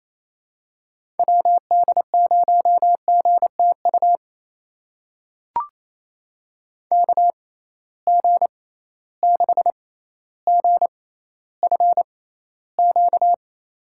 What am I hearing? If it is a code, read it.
Morse code, heard as a keyed tone, about 27 wpm, spaced as for 14 wpm, 700 Hz. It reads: WD0GTU KG6GFQ
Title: USA Call Signs - 6 Characters - 6x Character Spacing - 28wpm